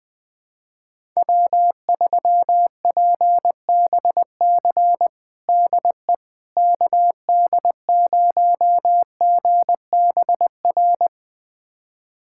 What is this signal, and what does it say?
Morse code, heard as a keyed tone, about 21 wpm, 700 Hz. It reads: W3PBC DE KD0GBR
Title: USA Callsign DE Callsign 20wpm